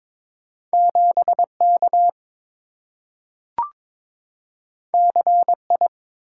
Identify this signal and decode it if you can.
Morse code, heard as a keyed tone, about 22 wpm, 700 Hz. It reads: 7K CI
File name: practice / 22wpm / Two Letter-Number Combinations - 22wpm